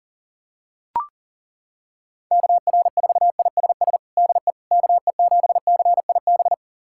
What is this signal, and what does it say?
Morse code, heard as a keyed tone, about 40 wpm, 700 Hz. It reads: KR4ISS DE KE7CIB
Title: USA Callsign DE Callsign - 6 Characters - 40wpm